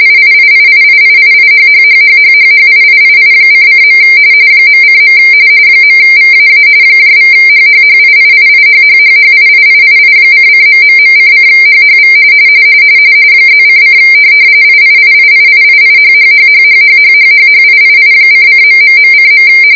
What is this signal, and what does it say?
RTTY, about 45.45 baud. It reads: IIIIIIIIIHHHHHHHIIII::. .:. .VAA =HHHIIIIHHHHIIIII::...::HHV +:: IIIIIIIIHHHHHHHHHHIII::...:.